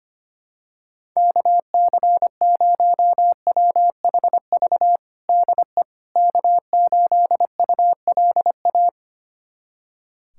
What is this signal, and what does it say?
Morse code, heard as a keyed tone, about 25 wpm, 700 Hz. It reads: KC0WHV DE K8ULA
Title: USA Callsign DE Callsign 25wpm